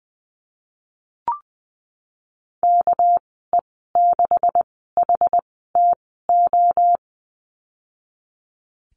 Morse code, heard as a keyed tone, about 20 wpm, spaced as for 14 wpm, 700 Hz. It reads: KE6HTO
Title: 245 Lesson 39g - Letter z - Call Signs 20wpm